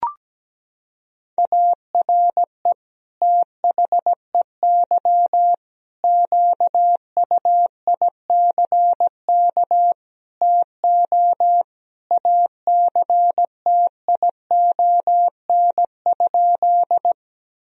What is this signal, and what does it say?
Morse code, heard as a keyed tone, about 17 wpm, 700 Hz. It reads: ARE THEY QUICK TO ACTION?
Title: Sentences from Top 1000 Words - Review Part 4 - 17wpm